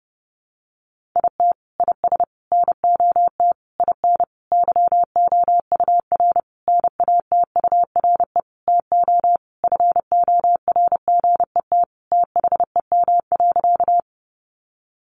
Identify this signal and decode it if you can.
Morse code, heard as a keyed tone, about 30 wpm, 700 Hz. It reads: IT IS NOT IN YOUR NATURE TO FORGET THEM.
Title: Sentences from Top 1200 Words 30wpm